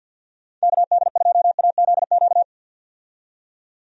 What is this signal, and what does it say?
Morse code, heard as a keyed tone, about 52 wpm, 700 Hz. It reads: KD2AZQ
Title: USA Call Signs - 6 Characters - 50wpm